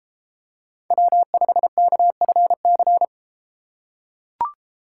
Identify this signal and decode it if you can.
Morse code, heard as a keyed tone, about 30 wpm, 700 Hz. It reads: W5KFC